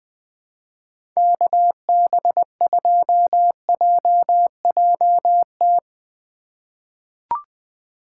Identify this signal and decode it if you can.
Morse code, heard as a keyed tone, about 20 wpm, 700 Hz. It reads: KB2JJT
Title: USA Call Signs - 6 Characters - 2x Character Spacing - 20wpm